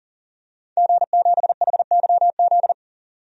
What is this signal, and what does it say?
Morse code, heard as a keyed tone, about 40 wpm, 700 Hz. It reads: G7HYZ
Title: DXCC-100 Call Signs - 5 Characters - 40wpm